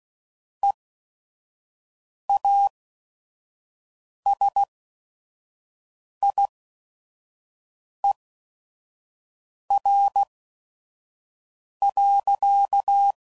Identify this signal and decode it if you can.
Morse code, heard as a keyed tone, about 16 wpm, spaced as for 5 wpm, 800 Hz. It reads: EASIER.
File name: CW Code Groups @5WPM / Technical Subjects / AND Gate